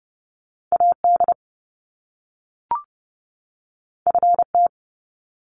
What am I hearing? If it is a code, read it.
Morse code, heard as a keyed tone, about 30 wpm, 700 Hz. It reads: AD FT
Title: Top Two Letter Combinations in Words 30wpm